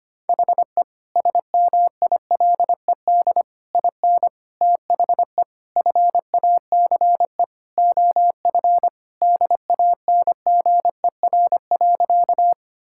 Morse code, heard as a keyed tone, about 25 wpm, 700 Hz. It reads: HE SMILED IN THE FACE OF DANGER.
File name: practice / 25wpm / Sentences with 4-Word Phrases - 25wpm